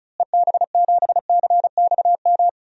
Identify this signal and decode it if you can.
Morse code, heard as a keyed tone, about 36 wpm, 700 Hz. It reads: EB7CXM
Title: DXCC-100 Call Signs - 6 Characters - Repeated 2x - 35wpm